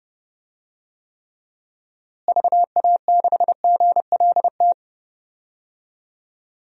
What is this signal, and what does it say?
Morse code, heard as a keyed tone, about 32 wpm, 700 Hz. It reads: VA6GLT
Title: DXCC-100 Call Signs - 6 Characters - Repeated 2x - 30wpm